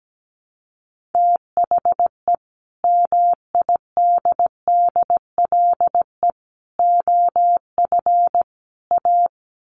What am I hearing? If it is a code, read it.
Morse code, heard as a keyed tone, about 17 wpm, 700 Hz. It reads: THE MIDDLE OF A
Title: Sets of 4 Words 17wpm